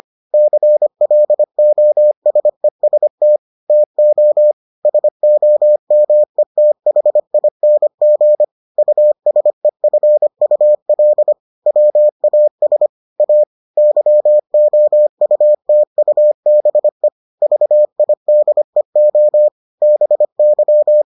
Morse code, heard as a keyed tone, about 25 wpm, 600 Hz. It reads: CLOSEST TO SOMETHING USEFUL WAS A YOUTUBE VIDEO BY